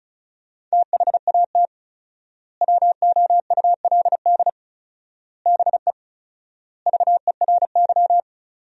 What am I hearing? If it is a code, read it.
Morse code, heard as a keyed tone, about 36 wpm, 700 Hz. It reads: THAT WOULD BE VERY
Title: 4x Word-Spacing - Sentences from Top 200 Words - 35wpm